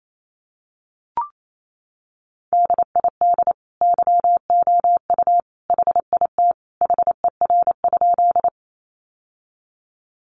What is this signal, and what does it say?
Morse code, heard as a keyed tone, about 28 wpm, 700 Hz. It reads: DID YOU HIT HER?
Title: Sentences from Top 1200 Words 28wpm